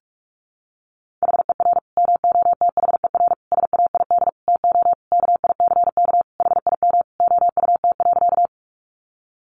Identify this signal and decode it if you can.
Morse code, heard as a keyed tone, about 44 wpm, 700 Hz. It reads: HER MOTHER SAID TO KICK HIM OUT.
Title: Sentences from Top 1300 Words 45wpm